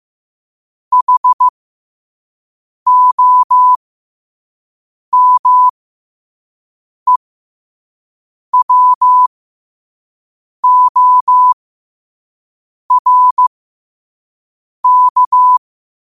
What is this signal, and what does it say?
Morse code, heard as a keyed tone, about 15 wpm, spaced as for 5 wpm, 1000 Hz. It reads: HOMEWORK
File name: qotdfiles / 5wpm